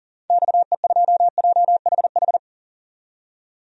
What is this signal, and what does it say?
Morse code, heard as a keyed tone, about 40 wpm, 700 Hz. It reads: XE2JHH
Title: DXCC-100 Call Signs - 6 Characters - 40wpm